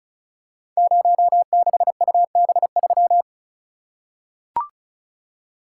Morse code, heard as a keyed tone, about 36 wpm, 700 Hz. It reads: 06UB3
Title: Code Groups - 5 Letter:Numbers - 35wpm